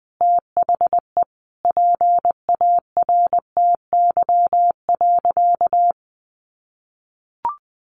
Morse code, heard as a keyed tone, about 20 wpm, 700 Hz. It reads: THE PARTY.
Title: Sentences from Top 1400 Words 20wpm